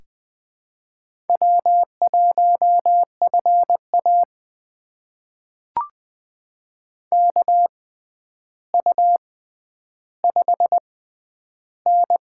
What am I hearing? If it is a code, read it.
Morse code, heard as a keyed tone, about 20 wpm, spaced as for 10 wpm, 700 Hz. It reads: W1FA KU5N
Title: USA Call Signs - 4 Characters - 6x Character Spacing - 20wpm